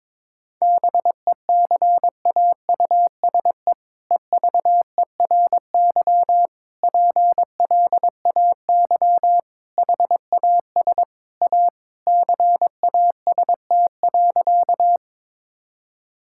Morse code, heard as a keyed tone, about 22 wpm, 700 Hz. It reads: BECAUSE EVERY PLAY HAS A CAST.
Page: morse-code-ninja-practice-22wpm